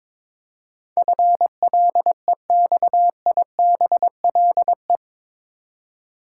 Morse code, heard as a keyed tone, about 21 wpm, 700 Hz. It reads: FLEXIBLE
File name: practice / 22wpm / Eight Letter Words - 22wpm